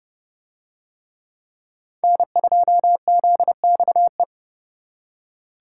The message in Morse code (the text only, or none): N2ZXE